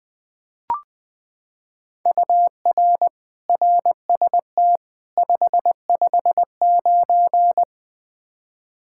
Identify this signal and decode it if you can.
Morse code, heard as a keyed tone, about 21 wpm, 700 Hz. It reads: UR RST 559
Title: Practice QSO - Exchange 1 - Beginning - 20wpm